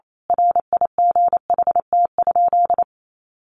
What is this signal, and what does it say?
Morse code, heard as a keyed tone, about 28 wpm, 700 Hz. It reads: RIGHT?
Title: A Sound Of Thunder 28wpm